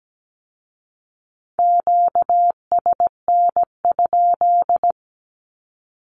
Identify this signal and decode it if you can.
Morse code, heard as a keyed tone, about 17 wpm, 700 Hz. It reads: QSN?
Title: Q-Codes with Spoken Meaning - 17wpm